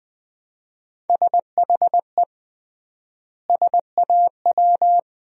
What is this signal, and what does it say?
Morse code, heard as a keyed tone, about 20 wpm, 700 Hz. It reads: SHE SAW
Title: 3x Word-Spacing - Sentences from Top 300 Words - 20wpm